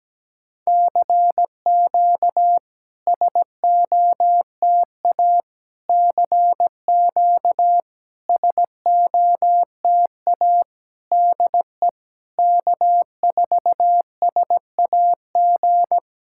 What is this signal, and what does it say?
Morse code, heard as a keyed tone, about 16 wpm, 700 Hz. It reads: CQ SOTA CQ SOTA DE K4SAG
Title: SOTA QSO Practice 17wpm